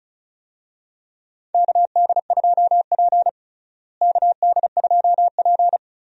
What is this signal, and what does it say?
Morse code, heard as a keyed tone, about 36 wpm, 700 Hz. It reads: KD2P KD2P
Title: USA Call Signs - 4 Characters - Repeated 3x - 35wpm